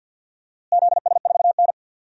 Morse code, heard as a keyed tone, about 50 wpm, 700 Hz. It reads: ZS4N
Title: DXCC-100 Call Signs - 4 Characters - Repeated 3x - 50wpm